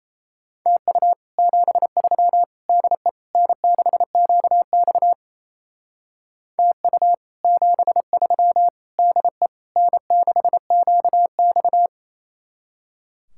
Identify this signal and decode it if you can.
Morse code, heard as a keyed tone, about 33 wpm, 700 Hz. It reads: TU 73 DE N6QX TU 73 DE N6QX
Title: Speed-Racing Common QSO Phrases - 28wpm